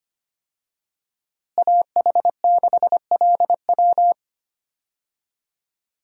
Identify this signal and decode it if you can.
Morse code, heard as a keyed tone, about 25 wpm, 700 Hz. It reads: AH6LW
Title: USA Call Signs - 5 Characters - Repeated 2x - 25wpm